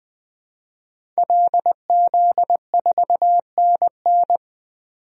Morse code, heard as a keyed tone, about 21 wpm, 700 Hz. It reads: LZ4NN
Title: DXCC-100 Call Signs - 5 Characters - 20wpm